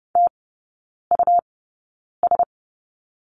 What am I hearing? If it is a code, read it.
Morse code, heard as a keyed tone, about 30 wpm, spaced as for 9 wpm, 700 Hz. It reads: TUS
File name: Lesson 18e - Letter m - Three Characters 30wpm